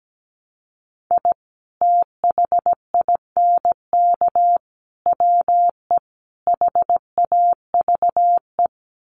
Morse code, heard as a keyed tone, about 17 wpm, 700 Hz. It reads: I THINK WE HAVE
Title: Sets of 4 Words 17wpm